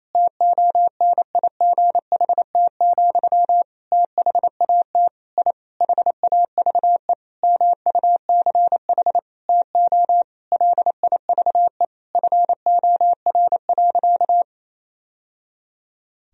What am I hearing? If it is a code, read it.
Morse code, heard as a keyed tone, about 28 wpm, 700 Hz. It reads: TONIGHT, THAT I HAVE MUCH TO LIVE FOR.
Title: Sentences from Top 1000 Words - Review Part 3 - 28wpm